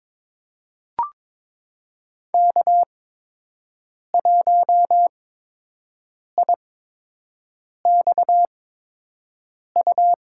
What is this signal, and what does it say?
Morse code, heard as a keyed tone, about 22 wpm, spaced as for 6 wpm, 700 Hz. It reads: K1IXU
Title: USA Call Signs - 5 Characters - 8x Character Spacing - 22wpm